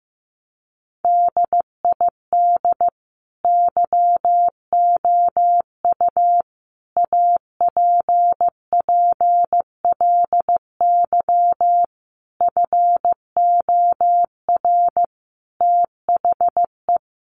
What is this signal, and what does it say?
Morse code, heard as a keyed tone, about 15 wpm, 700 Hz. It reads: DID YOU APPLY FOR THE
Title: Sentences from Top 1300 Words 15wpm